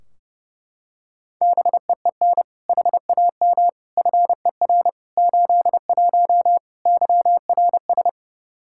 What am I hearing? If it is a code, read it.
Morse code, heard as a keyed tone, about 30 wpm, 700 Hz. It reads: BEEN HAM FER 81 YRS
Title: Practice QSO - Exchange 3 - More Info - 30wpm